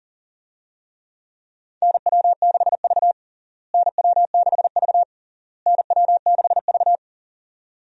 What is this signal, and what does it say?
Morse code, heard as a keyed tone, about 37 wpm, 700 Hz. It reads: NW6V NW6V NW6V
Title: USA Call Signs - 4 Characters - Repeated 3x - 40wpm